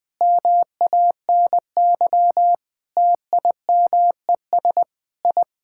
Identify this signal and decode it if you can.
Morse code, heard as a keyed tone, about 20 wpm, 700 Hz. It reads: MANY TIMES I
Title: Sentences with 8-Word Phrases - 20wpm